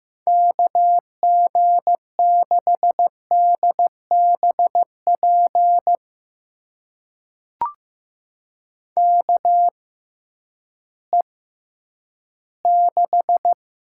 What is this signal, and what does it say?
Morse code, heard as a keyed tone, about 15 wpm, spaced as for 10 wpm, 700 Hz. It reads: KG6DBP KE6
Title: USA Call Signs - 6 Characters - 6x Character Spacing - 15wpm